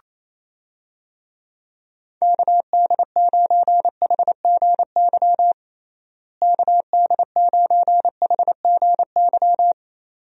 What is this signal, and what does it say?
Morse code, heard as a keyed tone, about 27 wpm, 700 Hz. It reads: KD9HGY KD9HGY
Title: USA Call Signs - 6 Characters - Repeated 2x - 28wpm